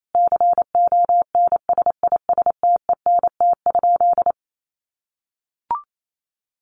Morse code, heard as a keyed tone, about 28 wpm, 700 Hz. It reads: CONSISTENT?
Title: Sentences from Top 1500 Words 28wpm